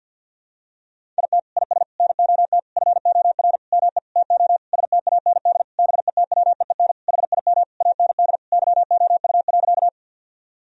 Morse code, heard as a keyed tone, about 46 wpm, 700 Hz. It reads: IT IS NOT FOR ME TO STAND BETWEEN HIM AND YOU.
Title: Sentences from Top 1000 Words - Review Part 1 - 50wpm